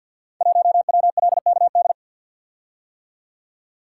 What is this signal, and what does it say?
Morse code, heard as a keyed tone, about 46 wpm, 700 Hz. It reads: 1WLKD
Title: Code Groups - 5 Letter:Numbers - 50wpm